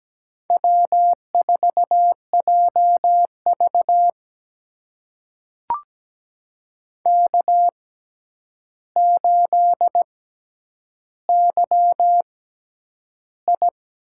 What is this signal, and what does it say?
Morse code, heard as a keyed tone, about 17 wpm, spaced as for 9 wpm, 700 Hz. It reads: W4JV K8YI
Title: USA Call Signs - 4 Characters - 6x Character Spacing - 17wpm